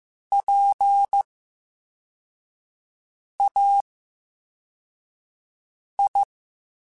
Morse code, heard as a keyed tone, about 15 wpm, spaced as for 4 wpm, 800 Hz. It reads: PAI
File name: Special INGs / Special INGs-mp3s / Slow ing @13